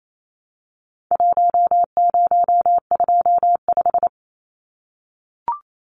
Morse code, heard as a keyed tone, about 28 wpm, 700 Hz. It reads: 1025
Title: Numbers - Contest Style - 4-Digit 28wpm